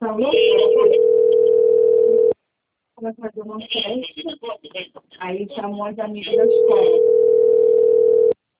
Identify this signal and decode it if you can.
Ringback tone (North America), 440 + 480 Hz, with speech.
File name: rg-1600-unknown- - - .14703